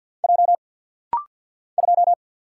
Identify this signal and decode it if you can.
Morse code, heard as a keyed tone, about 50 wpm, 700 Hz. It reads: J 2
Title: Single Letter-Number - Rapid-Fire 50wpm